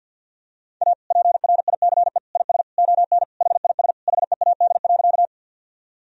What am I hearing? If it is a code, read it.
Morse code, heard as a keyed tone, about 50 wpm, 700 Hz. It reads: A PRICE IS ON HIS HEAD.
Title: Sentences from Top 1000 Words - Review Part 4 - 50wpm